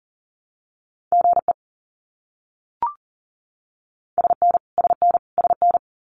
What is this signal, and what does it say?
Morse code, heard as a keyed tone, about 40 wpm, 700 Hz. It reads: GE SN SN SN
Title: Abbreviations Repeated 3x - 40wpm